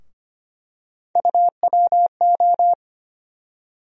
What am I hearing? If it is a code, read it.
Morse code, heard as a keyed tone, about 25 wpm, 700 Hz. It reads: UWO
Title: USA Call Sign Postfixes - 3 Letters 25wpm